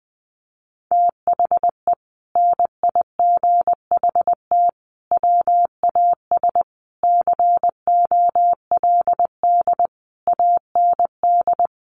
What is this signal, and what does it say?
Morse code, heard as a keyed tone, about 20 wpm, 700 Hz. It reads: THE NIGHT WAS COLD AND
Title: Sentences from Top 400 Words 20wpm